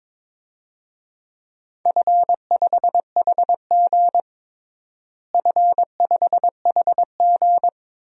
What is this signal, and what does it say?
Morse code, heard as a keyed tone, about 21 wpm, 700 Hz. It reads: F5HG F5HG
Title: DXCC-100 Call Signs - 4 Characters - Repeated 3x - 22wpm